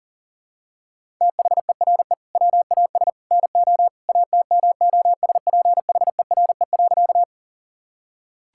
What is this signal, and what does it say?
Morse code, heard as a keyed tone, about 40 wpm, 700 Hz. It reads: THERE WAS NO ATMOSPHERE.
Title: Sentences from Top 1600 Words 40wpm